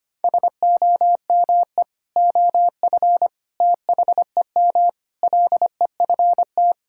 Morse code, heard as a keyed tone, about 25 wpm, 700 Hz. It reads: SOME OF THEM LEFT
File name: practice / 25wpm / Sentences with 3-Word Phrases - 25wpm